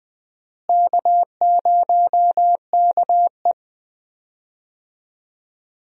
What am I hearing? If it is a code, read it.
Morse code, heard as a keyed tone, about 20 wpm, 700 Hz. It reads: K0KE
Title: USA Call Signs - 4 Characters - Repeated 2x - 20wpm